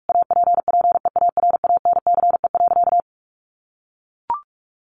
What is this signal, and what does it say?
Morse code, heard as a keyed tone, about 44 wpm, 700 Hz. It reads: APPEARANCE.
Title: Sentences from Top 1100 Words 45wpm